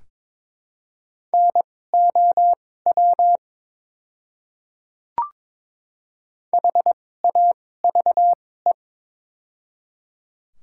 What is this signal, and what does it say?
Morse code, heard as a keyed tone, about 22 wpm, spaced as for 16 wpm, 700 Hz. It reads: NOW HAVE
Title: 2x Character Spacing - Top 100 Words - 22wpm